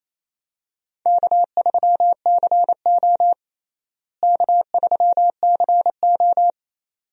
Morse code, heard as a keyed tone, about 27 wpm, 700 Hz. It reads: K3CO K3CO
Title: USA Call Signs - 4 Characters - Repeated 2x - 28wpm